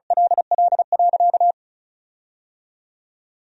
Morse code, heard as a keyed tone, about 36 wpm, 700 Hz. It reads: LL.
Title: 2x Word-Spacing - Sentences from Top 500 Words - 35wpm